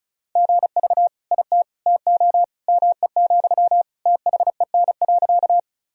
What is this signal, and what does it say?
Morse code, heard as a keyed tone, about 36 wpm, 700 Hz. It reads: GV IT TO ME, THEN.
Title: Sentences from Top 200 Words and Abbreviations 35wpm